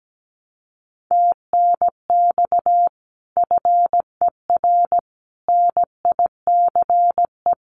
Morse code, heard as a keyed tone, about 17 wpm, 700 Hz. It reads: TNX FER NICE
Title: Common QSO Phrases 17wpm